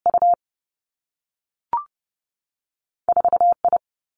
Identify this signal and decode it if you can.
Morse code, heard as a keyed tone, about 30 wpm, 700 Hz. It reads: U 4 I
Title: Two Challenging Letter-Number Combinations 30wpm